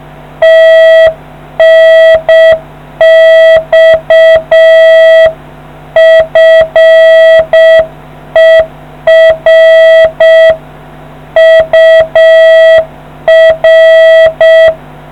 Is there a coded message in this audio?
Yes: Morse code from 0.4 to 14.7 s, heard as a keyed tone, about 7 wpm, 650 Hz. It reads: TNX FER UR